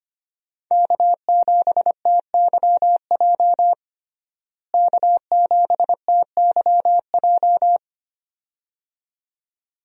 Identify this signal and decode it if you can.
Morse code, heard as a keyed tone, about 25 wpm, 700 Hz. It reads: K7TYJ K7TYJ
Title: USA Call Signs - 5 Characters - Repeated 2x - 25wpm